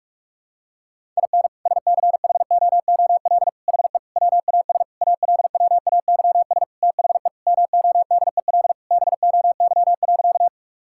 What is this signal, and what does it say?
Morse code, heard as a keyed tone, about 46 wpm, 700 Hz. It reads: IN SCHOOL HE WAS ALWAYS THE MODEL BOY.
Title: Sentences from Top 1000 Words - Review Part 5 - 45wpm